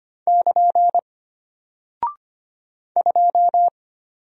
Morse code, heard as a keyed tone, about 25 wpm, 700 Hz. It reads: <KN> 2
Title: Single Character (Letter:Number:Punctuation:Prosign) Alt 25wpm